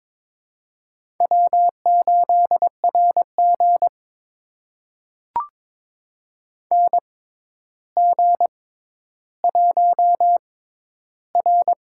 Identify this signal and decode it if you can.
Morse code, heard as a keyed tone, about 22 wpm, spaced as for 11 wpm, 700 Hz. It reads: W8RG NG1R
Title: USA Call Signs - 4 Characters - 6x Character Spacing - 22wpm